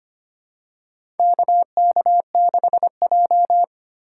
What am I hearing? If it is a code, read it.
Morse code, heard as a keyed tone, about 25 wpm, 700 Hz. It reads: KK6J
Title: USA Call Signs - 4 Characters - 3x Character Spacing - 25wpm